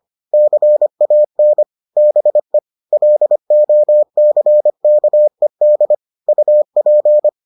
Morse code, heard as a keyed tone, about 25 wpm, 600 Hz. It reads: CAN BE LOCKED UP